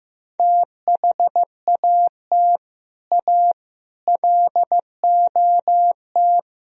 Morse code, heard as a keyed tone, about 15 wpm, 700 Hz. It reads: THAT A LOT
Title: Speed-Racing - Sets of 4 Words 15wpm